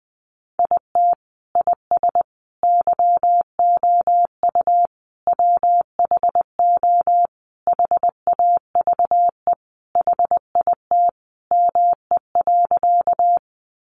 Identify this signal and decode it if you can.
Morse code, heard as a keyed tone, about 20 wpm, 700 Hz. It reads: IT IS YOU WHO HAVE HIT ME.
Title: Sentences from Top 1200 Words 20wpm